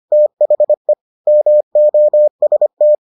Morse code, heard as a keyed tone, about 25 wpm, 600 Hz. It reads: THE MOST